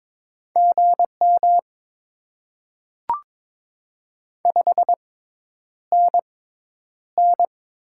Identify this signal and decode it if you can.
Morse code, heard as a keyed tone, about 22 wpm, spaced as for 10 wpm, 700 Hz. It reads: GM 5NN
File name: Top 20 QSO Elements - 6x Character Spacing - 22wpm